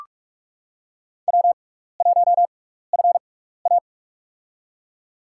Morse code, heard as a keyed tone, about 46 wpm, spaced as for 16 wpm, 700 Hz. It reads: W1FA
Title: USA Call Signs - 4 Characters - 6x Character Spacing - 45wpm